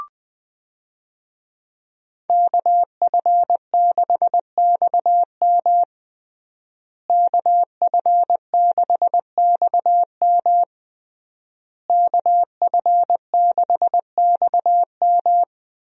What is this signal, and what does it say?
Morse code, heard as a keyed tone, about 21 wpm, 700 Hz. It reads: KF6XM KF6XM KF6XM
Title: USA Call Signs - 5 Characters - Repeated 3x - 20wpm